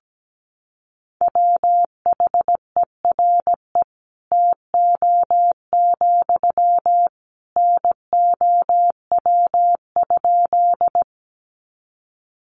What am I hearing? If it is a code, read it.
Morse code, heard as a keyed tone, about 17 wpm, 700 Hz. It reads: WHERE TO, NOW?
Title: Sentences from Top 200 Words 17wpm